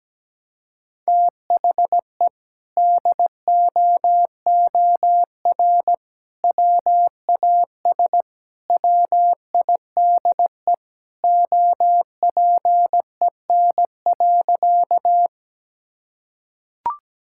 Morse code, heard as a keyed tone, about 17 wpm, 700 Hz. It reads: THE DOOR WAS WIDE OPEN.
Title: Sentences from Top 1000 Words - Review Part 6 - 17wpm